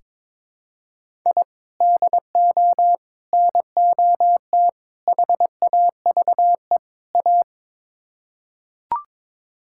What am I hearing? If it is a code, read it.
Morse code, heard as a keyed tone, about 22 wpm, 700 Hz. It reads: I DO NOT HAVE A
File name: practice / 22wpm / Build A Phrase - 5-Word Phrases - 22wpm